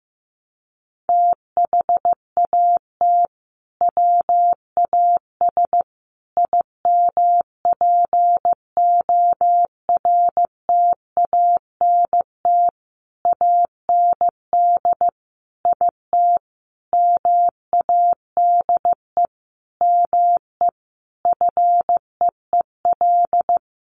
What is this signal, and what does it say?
Morse code, heard as a keyed tone, about 15 wpm, 700 Hz. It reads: THAT WAS IMPORTANT AND IT MADE ME FEEL
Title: Sentences from Top 600 Words 15wpm